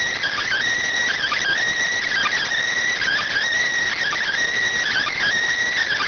Emergency Alert System (SAME) tones (header bursts), heard as harsh buzzing bursts. It is heard from 1.1 to 5.3 s.